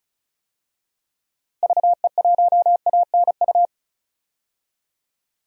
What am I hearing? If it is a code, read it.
Morse code, heard as a keyed tone, about 36 wpm, 700 Hz. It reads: VE1ANU
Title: DXCC-100 Call Signs - 6 Characters - 35wpm